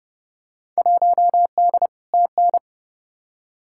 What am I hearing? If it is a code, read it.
Morse code, heard as a keyed tone, about 30 wpm, 700 Hz. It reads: 1D TN
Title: Field Day Exchanges - 30wpm